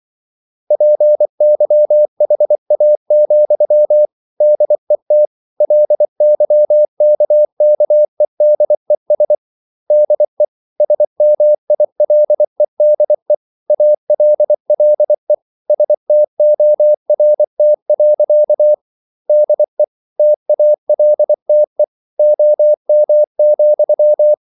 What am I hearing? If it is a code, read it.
Morse code, heard as a keyed tone, about 24 wpm, 600 Hz. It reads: PYHA, DET LYKKEDES DE SMILEDE ALLE STORT. DE TALTE OM,
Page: Ferie 24 wpm